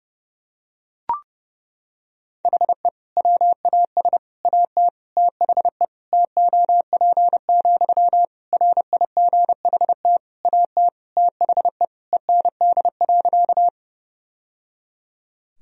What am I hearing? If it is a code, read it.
Morse code, heard as a keyed tone, about 30 wpm, 700 Hz. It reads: HE WAS AT THE TOP, RIGHT AT THE END.
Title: Sentences from Top 1000 Words - Review Part 2 - 30wpm